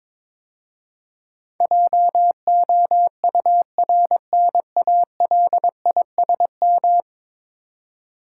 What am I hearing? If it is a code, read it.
Morse code, heard as a keyed tone, about 23 wpm, 700 Hz. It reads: JOURNALISM
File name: practice / 22wpm / 10 Letter Words - First 3 Letters Given - 22wpm